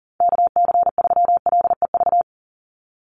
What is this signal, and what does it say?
Morse code, heard as a keyed tone, about 40 wpm, 700 Hz. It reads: KC3LEV
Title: USA Call Signs 40wpm